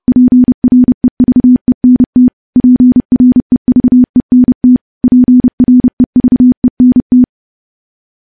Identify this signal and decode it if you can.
Morse code, heard as a keyed tone, about 30 wpm, 250 Hz. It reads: PREVENT PREVENT PREVENT
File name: morse / morse-training-30WPM